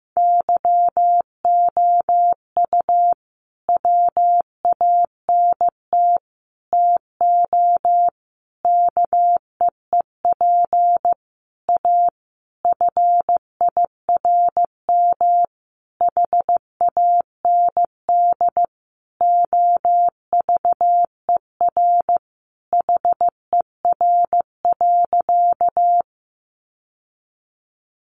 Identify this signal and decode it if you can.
Morse code, heard as a keyed tone, about 15 wpm, 700 Hz. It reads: YOU WANT TO KEEP A FIRM HAND OVER HER.
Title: Sentences from Top 1400 Words 15wpm